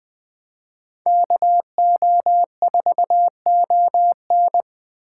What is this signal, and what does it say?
Morse code, heard as a keyed tone, about 20 wpm, 700 Hz. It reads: KO4ON
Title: USA Call Signs - 5 Characters - 2x Character Spacing - 20wpm